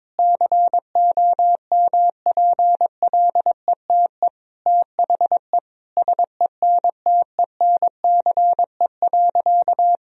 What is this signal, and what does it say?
Morse code, heard as a keyed tone, about 22 wpm, 700 Hz. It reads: COMPLETE THE SENTENCE.
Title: Sentences from Top 1000 Words - Review Part 1 - 22wpm